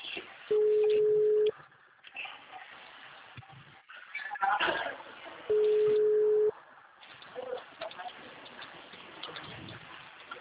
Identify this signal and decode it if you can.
Ringback tone (Europe, 425 Hz), with speech.
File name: out- -2004- - - .53